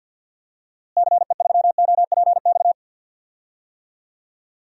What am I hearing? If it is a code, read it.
Morse code, heard as a keyed tone, about 47 wpm, 700 Hz. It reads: CE3OPX
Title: DXCC-100 Call Signs - 6 Characters - Repeated 2x - 50wpm